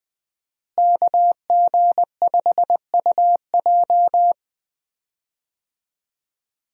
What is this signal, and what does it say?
Morse code, heard as a keyed tone, about 20 wpm, 700 Hz. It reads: KG5UJ
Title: USA Call Signs - 5 Characters - Repeated 3x - 20wpm